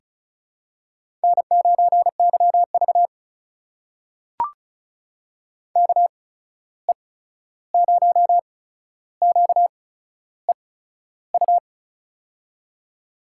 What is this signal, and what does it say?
Morse code, heard as a keyed tone, about 35 wpm, spaced as for 13 wpm, 700 Hz. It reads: N9YV KE0QEU
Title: USA Call Signs - 8x Character Spacing - 35wpm